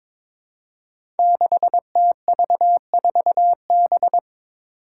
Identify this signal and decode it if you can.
Morse code, heard as a keyed tone, about 22 wpm, 700 Hz. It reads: 6TV4B
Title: Five Challenging Letter-Number Combinations 22wpm